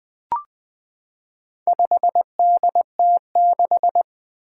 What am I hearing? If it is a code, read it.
Morse code, heard as a keyed tone, about 20 wpm, 700 Hz. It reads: 5DT6
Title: Four Challenging Letter-Number Combinations 20wpm